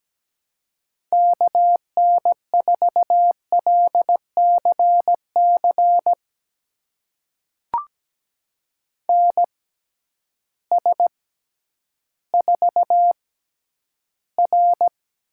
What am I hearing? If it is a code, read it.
Morse code, heard as a keyed tone, about 17 wpm, spaced as for 10 wpm, 700 Hz. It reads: KN4LCC NS4R
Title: USA Call Signs - 6x Character Spacing - 17wpm